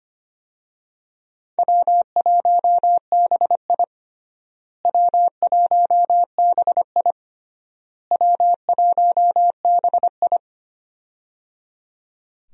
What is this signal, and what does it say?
Morse code, heard as a keyed tone, about 25 wpm, 700 Hz. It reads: W1BI W1BI W1BI
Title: USA Call Signs - 4 Characters - Repeated 3x - 25wpm